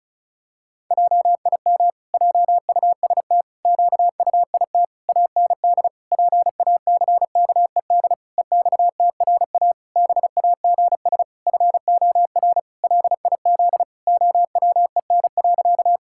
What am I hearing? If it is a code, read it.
Morse code, heard as a keyed tone, about 34 wpm, 700 Hz. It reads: JIM JUST QUIT AND PACKED EXTRA BAGS FOR LIZ OWEN.
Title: Pangram Challenge v2 - 35wpm